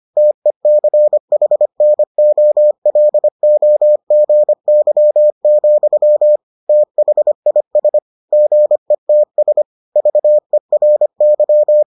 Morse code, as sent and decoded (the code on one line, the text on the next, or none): - . -.-. .... -. --- .-.. --- --. -.-- --..-- / - .... .. ... / --. - ... / ...- . .-. -.--
TECHNOLOGY, THIS GETS VERY